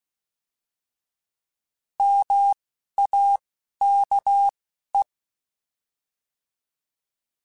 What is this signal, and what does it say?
Morse code, heard as a keyed tone, about 16 wpm, 800 Hz. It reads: M A K E
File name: Alphabetical / Alphabetical mp3s / Ms @10